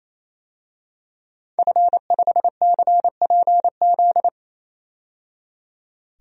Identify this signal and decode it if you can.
Morse code, heard as a keyed tone, about 27 wpm, 700 Hz. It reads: F5CPZ
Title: DXCC-100 Call Signs - 5 Characters - 28wpm